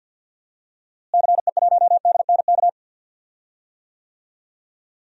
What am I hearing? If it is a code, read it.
Morse code, heard as a keyed tone, about 50 wpm, 700 Hz. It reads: CE1DNK
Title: DXCC-100 Call Signs - 6 Characters - Repeated 3x - 50wpm